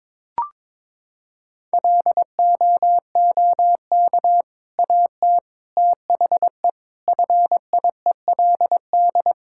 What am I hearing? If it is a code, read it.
Morse code, heard as a keyed tone, about 22 wpm, 700 Hz. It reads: LOOK AT THE FIELD
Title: Sentences from Top 1000 Words - Review Part 5 - 22wpm